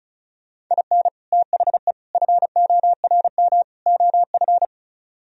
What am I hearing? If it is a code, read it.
Morse code, heard as a keyed tone, about 45 wpm, 700 Hz. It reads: IN THE FORM OF